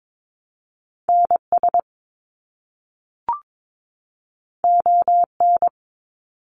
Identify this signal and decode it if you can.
Morse code, heard as a keyed tone, about 22 wpm, 700 Hz. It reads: NS ON
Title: Canadian Provinces and Territories 22wpm